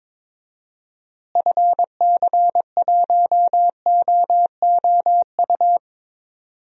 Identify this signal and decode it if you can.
Morse code, heard as a keyed tone, about 22 wpm, 700 Hz. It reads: FC1OOU
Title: DXCC-100 Call Signs - 6 Characters - Repeated 3x - 22wpm